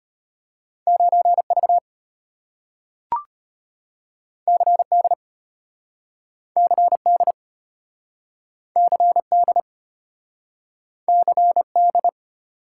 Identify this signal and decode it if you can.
Morse code, heard as a keyed tone, about 34 wpm, 700 Hz. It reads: 9V CD CD CD CD